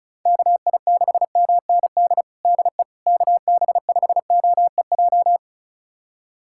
Morse code, heard as a keyed tone, about 34 wpm, 700 Hz. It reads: KI6MND DE KB5OEJ
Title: USA Callsign DE Callsign 35wpm